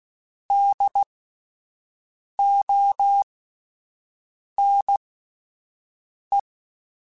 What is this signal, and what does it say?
Morse code, heard as a keyed tone, about 16 wpm, spaced as for 5 wpm, 800 Hz. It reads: DONE
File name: Spaced Alphas / Spaced Alphas mp3s / Ds sp @10